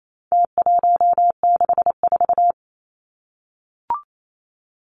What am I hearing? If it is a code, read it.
Morse code, heard as a keyed tone, about 28 wpm, 700 Hz. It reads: T164
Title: Numbers - Contest Style - 4-Digit 28wpm